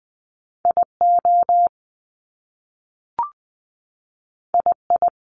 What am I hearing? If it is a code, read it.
Morse code, heard as a keyed tone, about 20 wpm, 700 Hz. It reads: I O I I
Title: Top Two Letter Combinations in Words 20wpm